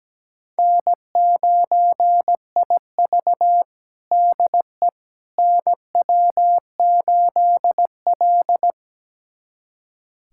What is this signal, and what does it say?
Morse code, heard as a keyed tone, about 17 wpm, 700 Hz. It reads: N9IV DE NW8L
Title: USA Callsign DE Callsign - 4 Characters - 17wpm